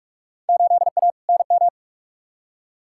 Morse code, heard as a keyed tone, about 46 wpm, 700 Hz. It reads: 8A NM
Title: Field Day Exchanges Encore - 45wpm